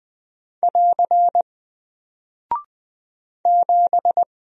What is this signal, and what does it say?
Morse code, heard as a keyed tone, about 20 wpm, 700 Hz. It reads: <AR> 7
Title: Single Character (Letter:Number:Punctuation:Prosign) Alt 20wpm